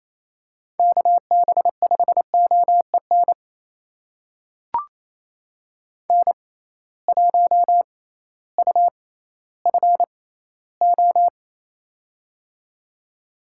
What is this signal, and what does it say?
Morse code, heard as a keyed tone, about 28 wpm, spaced as for 15 wpm, 700 Hz. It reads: KB5OEN N1UFO
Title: USA Call Signs - 6x Character Spacing - 28wpm